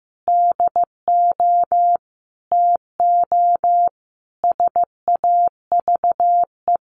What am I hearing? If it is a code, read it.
Morse code, heard as a keyed tone, about 15 wpm, 700 Hz. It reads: DO TO SAVE
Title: Sentences from Top 1100 Words 15wpm